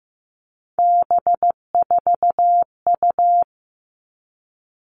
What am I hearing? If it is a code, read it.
Morse code, heard as a keyed tone, about 15 wpm, 700 Hz. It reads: B4U
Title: All Cut Numbers - 3-Digit 15wpm